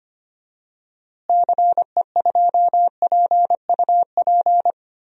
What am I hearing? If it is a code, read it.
Morse code, heard as a keyed tone, about 25 wpm, 700 Hz. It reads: CE2PUP
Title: DXCC-100 Call Signs - 6 Characters - Repeated 3x - 25wpm